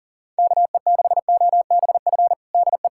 Morse code, heard as a keyed tone, about 40 wpm, 700 Hz. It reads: KE6OBF DE
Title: USA Callsign DE Callsign 40wpm